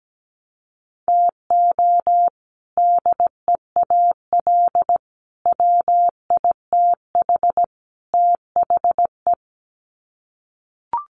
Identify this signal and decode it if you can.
Morse code, heard as a keyed tone, about 17 wpm, 700 Hz. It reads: TO DEAL WITH THE
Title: Sets of 4 Words 17wpm